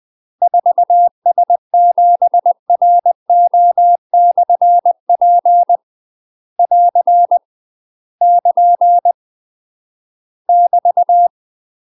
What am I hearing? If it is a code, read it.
Morse code, heard as a keyed tone, about 20 wpm, 700 Hz. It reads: 4S7RO/P <AR> <KN> <BT>
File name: UK MASTER QSOs TRAINING/20WPM